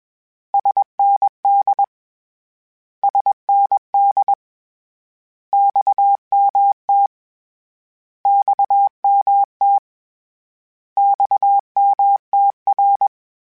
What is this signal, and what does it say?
Morse code, heard as a keyed tone, about 20 wpm, 800 Hz. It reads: SND SND XMT XMT XMTR
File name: Abreviations / Abreviations mp3s / Abreviations@20